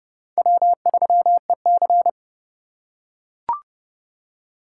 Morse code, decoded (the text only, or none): W3EC